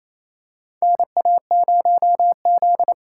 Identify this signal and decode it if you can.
Morse code, heard as a keyed tone, about 28 wpm, 700 Hz. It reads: NA0Z
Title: USA Call Signs - 4 Characters - 3x Character Spacing - 28wpm